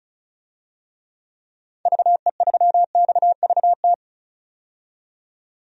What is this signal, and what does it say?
Morse code, heard as a keyed tone, about 34 wpm, 700 Hz. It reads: VE3XVT
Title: DXCC-100 Call Signs - 6 Characters - 35wpm